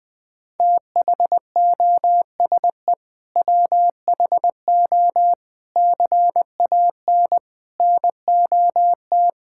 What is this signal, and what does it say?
Morse code, heard as a keyed tone, about 20 wpm, 700 Hz. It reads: THOSE WHO CAN NOT
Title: Sentences with 8-Word Phrases - Spoken Beginning - 20wpm